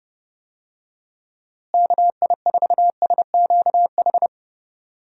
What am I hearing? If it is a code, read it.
Morse code, heard as a keyed tone, about 30 wpm, 700 Hz. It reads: KI4SQH
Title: USA Call Signs - 6 Characters - Repeated 3x - 30wpm